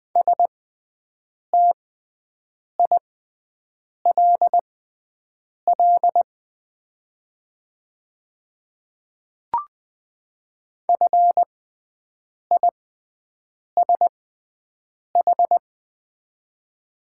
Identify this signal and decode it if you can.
Morse code, heard as a keyed tone, about 20 wpm, spaced as for 7 wpm, 700 Hz. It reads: STILL FISH
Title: 6x Character Spacing - Top 300 Words - 20wpm